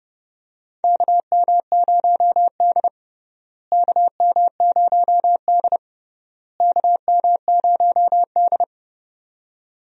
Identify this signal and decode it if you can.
Morse code, heard as a keyed tone, about 30 wpm, 700 Hz. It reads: KM0D KM0D KM0D
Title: USA Call Signs - 4 Characters - Repeated 3x - 30wpm